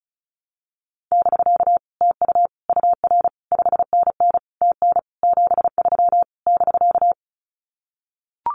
Morse code, heard as a keyed tone, about 34 wpm, 700 Hz. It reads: <BK> TU UR 5NN TN 73 <BK>
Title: Common QSO Phrases Encore 35wpm